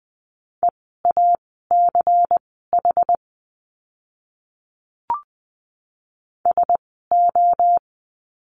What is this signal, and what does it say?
Morse code, heard as a keyed tone, about 20 wpm, spaced as for 15 wpm, 700 Hz. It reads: EACH SO
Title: 230 Lesson 37f - Letter x - Words 20wpm